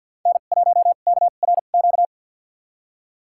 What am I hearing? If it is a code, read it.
Morse code, heard as a keyed tone, about 46 wpm, spaced as for 35 wpm, 700 Hz. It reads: N1KRQ
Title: USA Call Signs - 2x Character Spacing - 50wpm